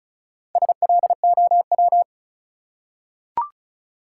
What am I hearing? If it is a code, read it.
Morse code, heard as a keyed tone, about 36 wpm, 700 Hz. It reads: SLOW
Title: Four Letter Words - 35wpm